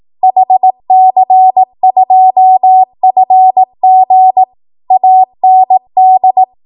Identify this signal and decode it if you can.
Morse code, heard as a keyed tone, about 18 wpm, 750 Hz. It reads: HC2FG AND
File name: Archive / 18 WPM / 18WPM